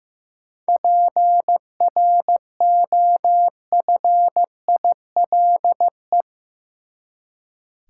Morse code, heard as a keyed tone, about 15 wpm, 700 Hz. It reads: PROFILE
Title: Seven Letter Words - 15wpm